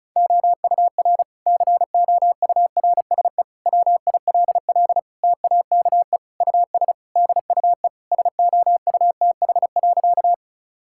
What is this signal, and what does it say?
Morse code, heard as a keyed tone, about 36 wpm, 700 Hz. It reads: OUR COURSE WILL TAKE US DUE SOUTH.
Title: Sentences from Top 1000 Words - Review Part 6 - 35wpm